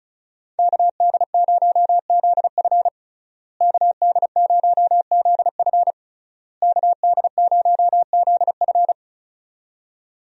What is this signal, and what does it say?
Morse code, heard as a keyed tone, about 34 wpm, 700 Hz. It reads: KD0ZF KD0ZF KD0ZF
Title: USA Call Signs - 5 Characters - Repeated 3x - 35wpm